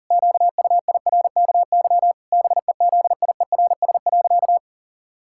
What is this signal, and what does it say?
Morse code, heard as a keyed tone, about 40 wpm, 700 Hz. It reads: QUIRKY BEZIERS.
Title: Pangram Challenge v2 - 40wpm